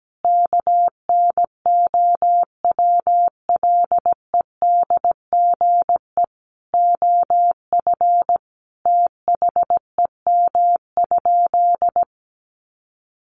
Morse code, heard as a keyed tone, about 17 wpm, 700 Hz. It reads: KNOWLEDGE OF THEM?
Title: Sentences from Top 600 Words 17wpm